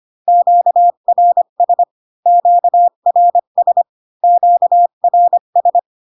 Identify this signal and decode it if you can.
Morse code, heard as a keyed tone, about 25 wpm, 700 Hz. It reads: QRS QRS QRS